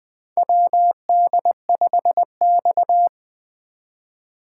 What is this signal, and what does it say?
Morse code, heard as a keyed tone, about 20 wpm, 700 Hz. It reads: WD5X
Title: USA Call Signs - 4 Characters - 2x Character Spacing - 20wpm